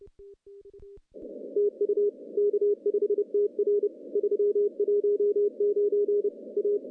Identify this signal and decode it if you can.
Morse code, heard as a keyed tone, about 30 wpm, 410 Hz. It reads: TU K5TR 319 A
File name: contests / CW-SS-k5tr- -1400